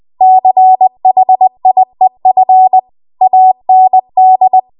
Morse code, heard as a keyed tone, about 20 wpm, 750 Hz. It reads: CHIEF AND
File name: Archive / 20 WPM / 20WPM